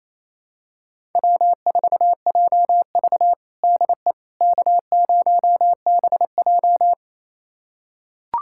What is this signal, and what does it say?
Morse code, heard as a keyed tone, about 28 wpm, 700 Hz. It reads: W4JV DE K0BJ
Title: USA Callsign DE Callsign - 4 Characters - 28wpm